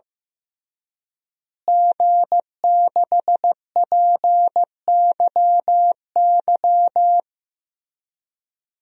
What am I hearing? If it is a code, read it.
Morse code, heard as a keyed tone, about 15 wpm, 700 Hz. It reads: G6PYY
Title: DXCC-100 Call Signs - 5 Characters - Repeated 3x - 15wpm